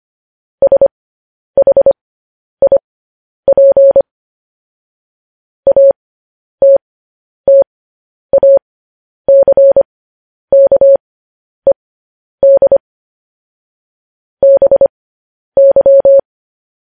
Morse code, heard as a keyed tone, about 25 wpm, spaced as for 10 wpm, 550 Hz. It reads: SHIP ATTACKED BY